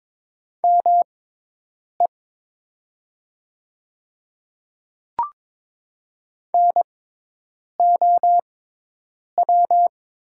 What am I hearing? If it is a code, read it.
Morse code, heard as a keyed tone, about 22 wpm, spaced as for 8 wpm, 700 Hz. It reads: ME NOW
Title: 6x Character Spacing - Top 200 Words - 22wpm